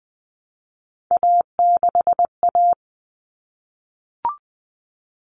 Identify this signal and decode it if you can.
Morse code, heard as a keyed tone, about 20 wpm, 700 Hz. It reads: A6A
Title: Common Cut Numbers - 3-Digit 20wpm